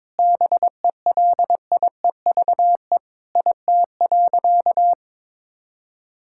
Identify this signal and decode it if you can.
Morse code, heard as a keyed tone, about 22 wpm, 700 Hz. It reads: BELIEVE IT.
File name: practice / 22wpm / Sentences with 4-Word Phrases - 22wpm